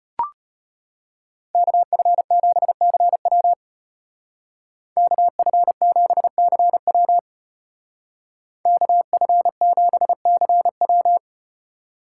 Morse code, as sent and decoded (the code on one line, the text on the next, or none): -.- ..-. --... -.-. .-- / -.- ..-. --... -.-. .-- / -.- ..-. --... -.-. .--
KF7CW KF7CW KF7CW